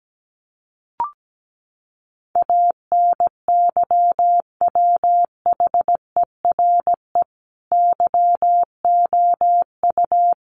Morse code, heard as a keyed tone, about 17 wpm, 700 Hz. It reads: ANYWHERE YOU
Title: Sentences from Top 1400 Words 17wpm